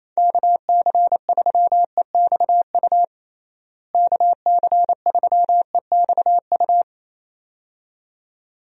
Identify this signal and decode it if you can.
Morse code, heard as a keyed tone, about 28 wpm, 700 Hz. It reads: KC3EXU KC3EXU
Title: USA Call Signs - 6 Characters - Repeated 2x - 28wpm